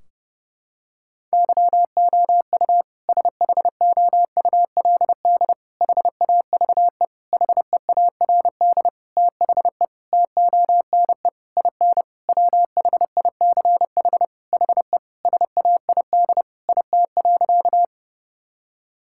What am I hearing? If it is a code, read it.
Morse code, heard as a keyed tone, about 30 wpm, 700 Hz. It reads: YOU SHOULD HAVE HEARD THE TONE IN WHICH HE SAID IT.
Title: Sentences from Top 1600 Words 30wpm